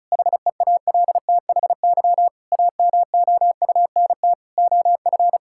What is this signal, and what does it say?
Morse code, heard as a keyed tone, about 36 wpm, 700 Hz. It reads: HEALTHY AMOUNT OF
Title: Sentences from Top 1000 Words - Review Part 5 - 35wpm